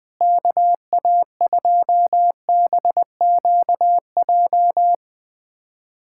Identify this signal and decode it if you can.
Morse code, heard as a keyed tone, about 20 wpm, 700 Hz. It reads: KA2BQJ
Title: USA Callsign DE Callsign - 6 Characters - 20wpm